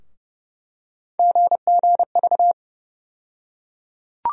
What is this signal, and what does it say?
Morse code, heard as a keyed tone, about 30 wpm, 700 Hz. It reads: GGV
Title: USA Call Sign Postfixes - 3 Letters 30wpm